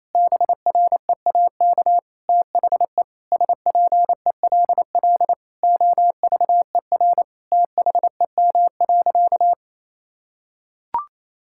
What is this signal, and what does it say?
Morse code, heard as a keyed tone, about 28 wpm, 700 Hz. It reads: BREAK THE SPELL OVER THEM.
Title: Sentences from Top 1000 Words - Review Part 6 - 28wpm